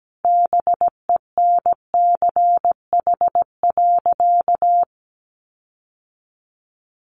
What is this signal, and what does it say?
Morse code, heard as a keyed tone, about 17 wpm, 700 Hz. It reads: BENCH.
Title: Sentences from Top 1400 Words 17wpm